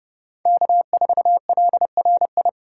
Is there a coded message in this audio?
Yes: Morse code from 0.5 to 2.5 s, heard as a keyed tone, about 30 wpm, 700 Hz. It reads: K4LRI